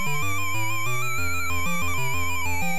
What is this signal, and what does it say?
There is no code.